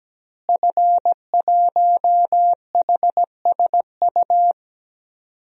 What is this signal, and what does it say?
Morse code, heard as a keyed tone, about 17 wpm, 700 Hz. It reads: F1HSU
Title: DXCC-100 Call Signs - 5 Characters - Repeated 2x - 17wpm